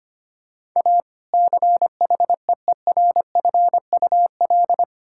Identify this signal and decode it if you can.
Morse code, heard as a keyed tone, about 25 wpm, 700 Hz. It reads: A CHEERFUL
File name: practice / 25wpm / Difficult Sentences with Hints - 25wpm